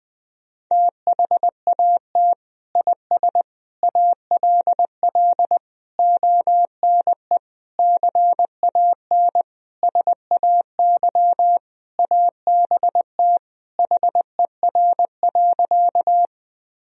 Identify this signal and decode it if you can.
Morse code, heard as a keyed tone, about 20 wpm, 700 Hz. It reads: THAT IS ALL ONE CAN SAY ABT HER.
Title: Sentences from Top 200 Words and Abbreviations 20wpm